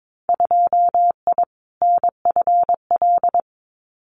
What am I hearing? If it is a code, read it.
Morse code, heard as a keyed tone, about 22 wpm, 700 Hz. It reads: 2I NFL
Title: Winter Field Day Exchanges 22wpm